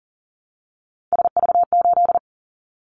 Morse code, heard as a keyed tone, about 40 wpm, 700 Hz. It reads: SV8
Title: Callsign Prefixes for Top 50 DXCC - 40wpm